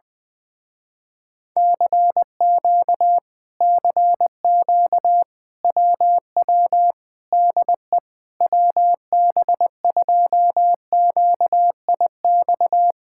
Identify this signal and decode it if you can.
Morse code, heard as a keyed tone, about 20 wpm, 700 Hz. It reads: CQ CQ WW DE WB2QIX
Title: Speed-Racing Common QSO Phrases Encore - 17wpm